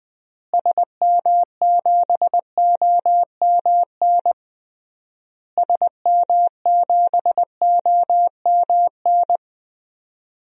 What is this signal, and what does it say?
Morse code, heard as a keyed tone, about 20 wpm, 700 Hz. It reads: SM7OMN SM7OMN
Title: DXCC-100 Call Signs - 6 Characters - Repeated 3x - 20wpm